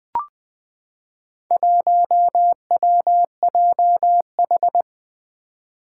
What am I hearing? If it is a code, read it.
Morse code, heard as a keyed tone, about 20 wpm, 700 Hz. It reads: 1WJH
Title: Four Challenging Letter-Number Combinations 20wpm